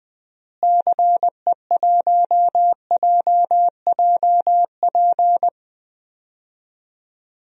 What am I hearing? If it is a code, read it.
Morse code, heard as a keyed tone, about 21 wpm, 700 Hz. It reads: CE1JJP
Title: DXCC-100 Call Signs - 6 Characters - Repeated 2x - 20wpm